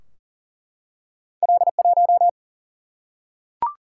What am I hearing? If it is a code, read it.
Morse code, heard as a keyed tone, about 40 wpm, 700 Hz. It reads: L 1
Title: Single-Letter Single-Number 40wpm